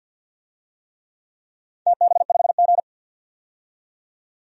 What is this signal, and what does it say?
Morse code, heard as a keyed tone, about 52 wpm, 700 Hz. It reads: TB5G
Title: DXCC-100 Call Signs - 4 Characters - 50wpm